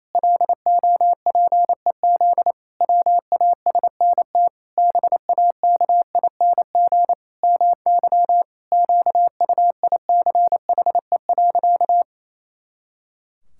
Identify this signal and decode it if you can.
Morse code, heard as a keyed tone, about 28 wpm, 700 Hz. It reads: LOPEZ WASNT BAKING MY QUICHE.
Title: Pangram Challenge v2 - 28wpm